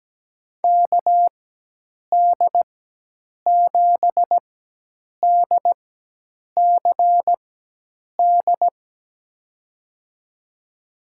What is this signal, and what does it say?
Morse code, heard as a keyed tone, about 17 wpm, spaced as for 8 wpm, 700 Hz. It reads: KD7DCD
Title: USA Call Signs - 6 Characters - 4x Character Spacing - 17wpm